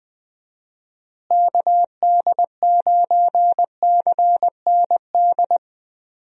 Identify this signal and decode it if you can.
Morse code, heard as a keyed tone, about 20 wpm, 700 Hz. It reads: KD9CND
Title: USA Call Signs - 6 Characters - Repeated 2x - 20wpm